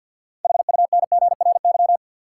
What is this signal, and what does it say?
Morse code, heard as a keyed tone, about 53 wpm, 700 Hz. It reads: HUNGRY
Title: Sentences from Top 1400 Words - New Word Drill - 50wpm